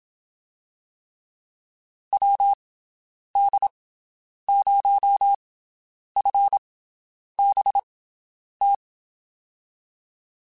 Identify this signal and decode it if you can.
Morse code, heard as a keyed tone, about 27 wpm, 800 Hz. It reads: W D 0 F B T
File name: (spaced) / QSOs Spaced-mp3s / WD0FBT 1 @25 sp